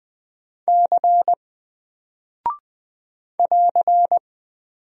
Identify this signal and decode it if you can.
Morse code, heard as a keyed tone, about 20 wpm, 700 Hz. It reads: C <AR>
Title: Single Character (Letter:Number:Punctuation:Prosign) Alt 20wpm